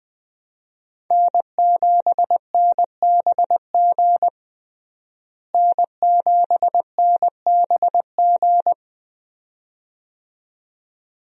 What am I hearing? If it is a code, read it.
Morse code, heard as a keyed tone, about 20 wpm, 700 Hz. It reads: N7NBG N7NBG
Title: USA Call Signs - 5 Characters - Repeated 2x - 20wpm